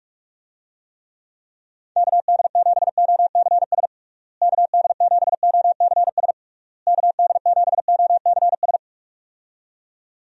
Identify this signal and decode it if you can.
Morse code, heard as a keyed tone, about 46 wpm, 700 Hz. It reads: KD7OCS KD7OCS KD7OCS
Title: USA Call Signs - 6 Characters - Repeated 3x - 45wpm